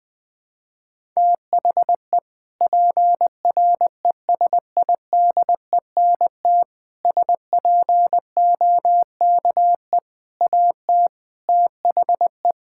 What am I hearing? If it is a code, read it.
Morse code, heard as a keyed tone, about 20 wpm, 700 Hz. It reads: THE PRESIDENT SPOKE AT THE
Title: Sentences with 4-Word Phrases - Spoken Beginning - 20wpm